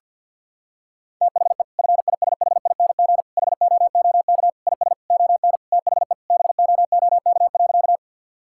Morse code, heard as a keyed tone, about 50 wpm, 700 Hz. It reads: THE FISHING HOOK IS ON THE BOOK.
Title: Sentences from Top 1600 Words 50wpm